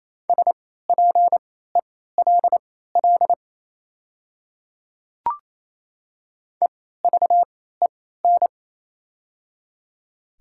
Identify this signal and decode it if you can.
Morse code, heard as a keyed tone, about 28 wpm, spaced as for 16 wpm, 700 Hz. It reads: SPELL EVEN
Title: 3x Character Spacing - Top 200 Words - 28wpm